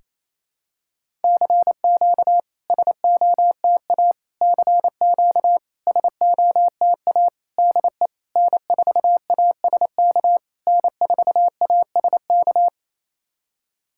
Morse code, heard as a keyed tone, about 29 wpm, 700 Hz. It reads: CQ SOTA CQ SOTA DE N4ASK N4ASK
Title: SOTA QSO Practice 28wpm